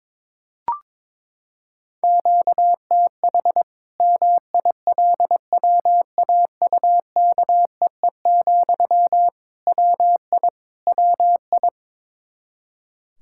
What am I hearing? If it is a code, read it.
Morse code, heard as a keyed tone, about 22 wpm, 700 Hz. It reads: QTH MILWAUKEE, WI WI
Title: K1USN SST QSO Practice 22wpm